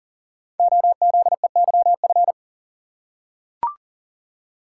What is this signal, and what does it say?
Morse code, heard as a keyed tone, about 39 wpm, 700 Hz. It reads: OZEYF
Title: Code Groups - 5 Letter:Numbers - 40wpm